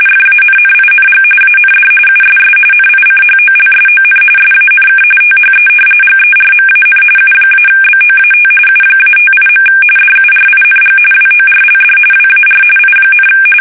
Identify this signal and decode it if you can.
RTTY, about 45.45 baud. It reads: FANNING AND USS NICHOLSON SANK THE GERMAN U-BOAT U-58. DURING WORLD WAR I,
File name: fsk850